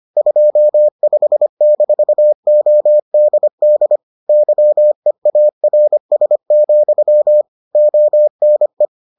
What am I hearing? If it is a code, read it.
Morse code, heard as a keyed tone, about 25 wpm, 600 Hz. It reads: 25-ODD YEARS, ONE